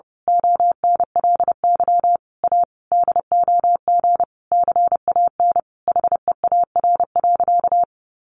Morse code, heard as a keyed tone, about 28 wpm, 700 Hz. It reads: ONLY A DOG CAN HEAR.
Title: A Sound Of Thunder 30wpm